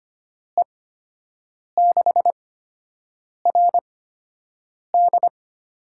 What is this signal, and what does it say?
Morse code, heard as a keyed tone, about 25 wpm, spaced as for 7 wpm, 700 Hz. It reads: E6RD
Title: USA Call Signs - 8x Character Spacing - 25wpm